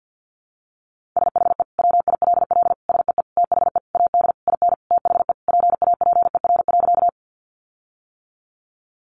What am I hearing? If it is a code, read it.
Morse code, heard as a keyed tone, about 50 wpm, 700 Hz. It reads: SHE WILL SEE THE AD IN THE PAPER.
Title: Sentences from Top 1300 Words 50wpm